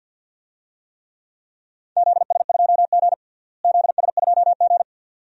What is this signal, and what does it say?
Morse code, heard as a keyed tone, about 46 wpm, 700 Hz. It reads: ZS2G ZS2G
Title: DXCC-100 Call Signs - 4 Characters - Repeated 2x - 50wpm